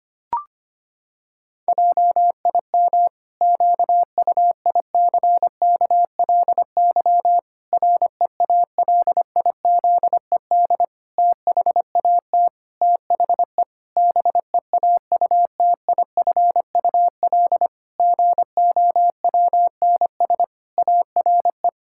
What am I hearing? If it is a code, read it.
Morse code, heard as a keyed tone, about 25 wpm, 700 Hz. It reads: JIM QUICKLY REALIZED THAT THE BEAUTIFUL GOWNS ARE
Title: Pangram Challenge v2 - 25wpm